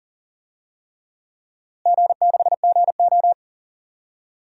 Morse code, heard as a keyed tone, about 40 wpm, 700 Hz. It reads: G6GO
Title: DXCC-100 Call Signs - 4 Characters - Repeated 3x - 40wpm